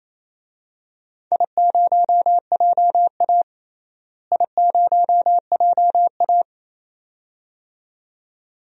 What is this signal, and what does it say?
Morse code, heard as a keyed tone, about 28 wpm, 700 Hz. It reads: I0JA I0JA
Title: DXCC-100 Call Signs - 4 Characters - Repeated 2x - 28wpm